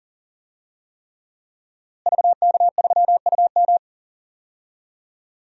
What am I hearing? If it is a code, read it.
Morse code, heard as a keyed tone, about 40 wpm, 700 Hz. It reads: VK3UM
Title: DXCC-100 Call Signs - 5 Characters - Repeated 3x - 40wpm